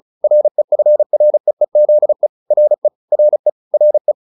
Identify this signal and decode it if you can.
Morse code, heard as a keyed tone, about 34 wpm, 600 Hz. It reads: REFREEZE RE RE RE